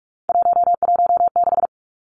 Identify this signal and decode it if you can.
Morse code, heard as a keyed tone, about 46 wpm, 700 Hz. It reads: 116
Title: Three Challenging Letter-Number Combinations 45wpm